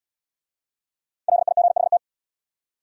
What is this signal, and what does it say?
Morse code, heard as a keyed tone, about 71 wpm, 700 Hz. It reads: LIGHT